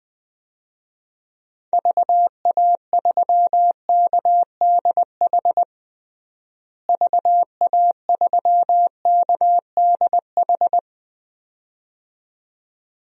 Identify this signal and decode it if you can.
Morse code, heard as a keyed tone, about 21 wpm, 700 Hz. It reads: VA3KDH VA3KDH
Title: DXCC-100 Call Signs - 6 Characters - Repeated 2x - 20wpm